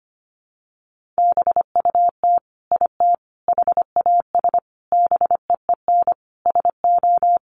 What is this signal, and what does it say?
Morse code, heard as a keyed tone, about 25 wpm, 700 Hz. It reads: BUT IT HAS BEEN SO
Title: Sentences from Top 100 Words 25wpm